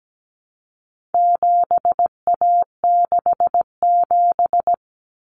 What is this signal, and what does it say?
Morse code, heard as a keyed tone, about 17 wpm, 700 Hz. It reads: 7A67
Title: Common Cut Numbers - 4-Digit 17wpm